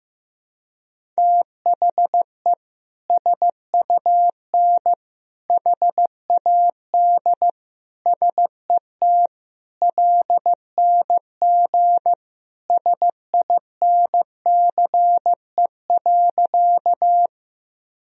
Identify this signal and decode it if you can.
Morse code, heard as a keyed tone, about 15 wpm, 700 Hz. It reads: THE SUN HAD SET LNG SINCE.
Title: Sentences from Top 300 Words and Abbreviations 15wpm